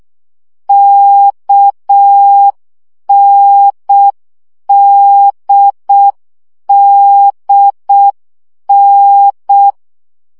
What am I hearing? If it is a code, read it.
Morse code, heard as a keyed tone, about 6 wpm, 800 Hz. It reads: KNDDN
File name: CW gyakorlas n d k 5 baud